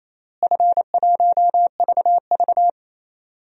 Morse code, heard as a keyed tone, about 28 wpm, 700 Hz. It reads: F1VV
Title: DXCC-100 Call Signs - 4 Characters - Repeated 3x - 28wpm